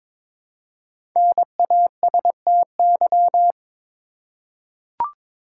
Morse code, heard as a keyed tone, about 22 wpm, 700 Hz. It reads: NASTY
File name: 5 Letter Words - First 3 Letters Given - 22wpm